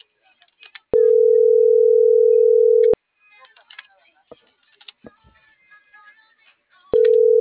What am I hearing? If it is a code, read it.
Ringback tone (North America), 440 + 480 Hz.
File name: CENTRAL NMP / out melanie.diaznmp - .46052